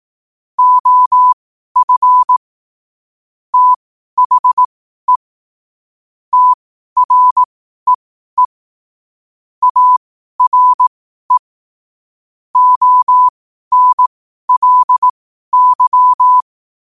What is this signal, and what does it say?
Morse code, heard as a keyed tone, about 18 wpm, spaced as for 12 wpm, 1000 Hz. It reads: OF THE TREE ARE ONLY
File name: qotdfiles / 12wpm